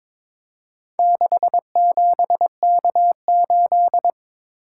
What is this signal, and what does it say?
Morse code, heard as a keyed tone, about 22 wpm, 700 Hz. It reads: 67K8
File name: practice / 22wpm / Four Letter-Number Combinations - 22wpm